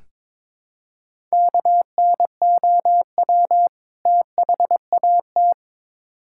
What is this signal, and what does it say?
Morse code, heard as a keyed tone, about 23 wpm, 700 Hz. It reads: KNOW THAT
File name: Build A Phrase - 2-Word Phrases - 22wpm